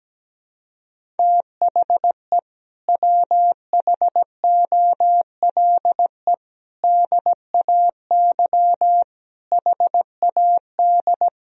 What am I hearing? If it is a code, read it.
Morse code, heard as a keyed tone, about 17 wpm, 700 Hz. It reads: THE WHOLE DAY HAD
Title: Sentences from Top 500 Words and Abbreviations 17wpm